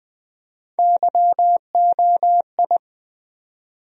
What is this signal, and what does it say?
Morse code, heard as a keyed tone, about 20 wpm, 700 Hz. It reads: YOI
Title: USA Call Sign Postfixes - 3 Letters 20wpm